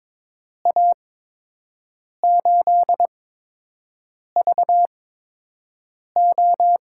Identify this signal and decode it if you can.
Morse code, heard as a keyed tone, about 22 wpm, spaced as for 6 wpm, 700 Hz. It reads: A8VO
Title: USA Call Signs - 5 Characters - 8x Character Spacing - 22wpm